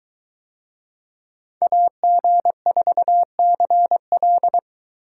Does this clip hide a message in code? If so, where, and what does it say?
Yes: Morse code from 1.6 to 4.6 s, heard as a keyed tone, about 21 wpm, 700 Hz. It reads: AG4CL